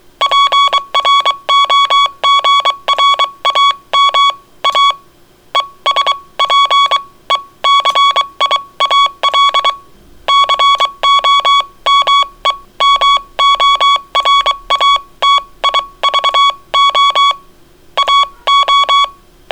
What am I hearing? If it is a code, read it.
Morse code, heard as a keyed tone, about 22 wpm, spaced as for 19 wpm, 1120 Hz. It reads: PROGRAMA ESPECIAL COMEMORATIVO AO